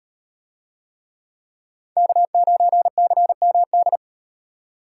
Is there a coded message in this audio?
Yes: Morse code from 2.0 to 4.0 s, heard as a keyed tone, about 33 wpm, 700 Hz. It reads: K9CMD